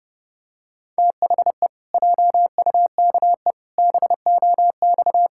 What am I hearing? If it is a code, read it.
Morse code, heard as a keyed tone, about 30 wpm, 700 Hz. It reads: THE JUKE BOX
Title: Pangram Challenge v2 - 30wpm